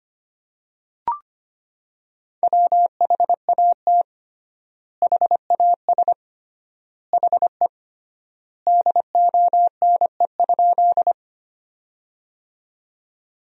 Morse code, heard as a keyed tone, about 25 wpm, 700 Hz. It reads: WHAT HAS HE DONE?
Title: 3x Word-Spacing - Sentences from Top 500 Words - 25wpm